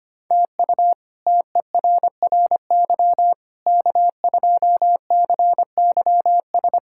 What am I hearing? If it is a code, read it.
Morse code, heard as a keyed tone, about 25 wpm, 700 Hz. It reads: TU TERRY K2CYS
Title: K1USN SST QSO Practice - 25wpm